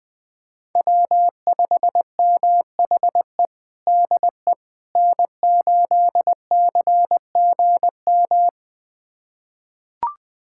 Morse code, heard as a keyed tone, about 20 wpm, 700 Hz. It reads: W5MHE DE N8CGM
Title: USA Callsign DE Callsign - 5 Characters - 20wpm